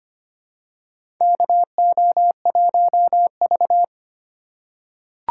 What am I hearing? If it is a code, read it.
Morse code, heard as a keyed tone, about 25 wpm, 700 Hz. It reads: KO1V
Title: USA Call Signs - 4 Characters - 8x Character Spacing - 25wpm